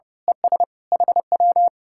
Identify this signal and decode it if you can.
Morse code, heard as a keyed tone, about 29 wpm, 700 Hz. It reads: ES HW
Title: Speed-Racing Common QSO Phrases Encore - 30wpm